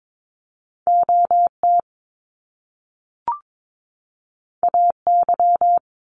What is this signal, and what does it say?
Morse code, heard as a keyed tone, about 22 wpm, 700 Hz. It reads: OT AY
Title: Top Two Letter Combinations in Words 22wpm